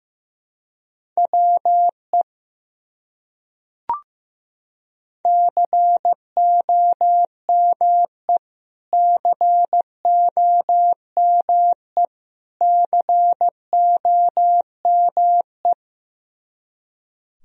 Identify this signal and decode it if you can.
Morse code, heard as a keyed tone, about 15 wpm, 700 Hz. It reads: WE COME COME COME
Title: Top 100 Words Repeated 15wpm